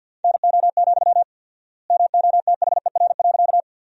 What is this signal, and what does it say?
Morse code, heard as a keyed tone, about 50 wpm, 700 Hz. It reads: NO, MOTHER.
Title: 4x Word-Spacing - Sentences from Top 200 Words - 50wpm